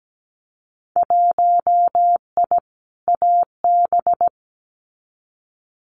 Morse code, heard as a keyed tone, about 17 wpm, 700 Hz. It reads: 1I AB
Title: Winter Field Day Exchanges Encore 17wpm